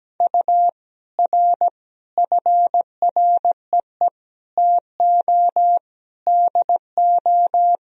Morse code, heard as a keyed tone, about 17 wpm, 700 Hz. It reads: U R FREE TO DO
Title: Sentences from Top 500 Words and Abbreviations 17wpm